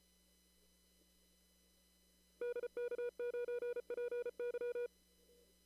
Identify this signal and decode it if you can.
Morse code, heard as a keyed tone, about 34 wpm, 470 Hz. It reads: DK9PY
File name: CW / kt5j wpx cw-op-k5pi